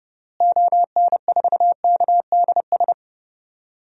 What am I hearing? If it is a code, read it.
Morse code, heard as a keyed tone, about 30 wpm, 700 Hz. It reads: ON4KDS
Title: DXCC-100 Call Signs - 6 Characters - Repeated 3x - 30wpm